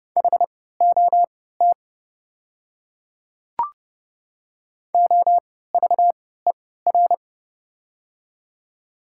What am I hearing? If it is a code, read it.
Morse code, heard as a keyed tone, about 31 wpm, spaced as for 17 wpm, 700 Hz. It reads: HOT OVER
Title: 3x Character Spacing - Top 300 Words - 30wpm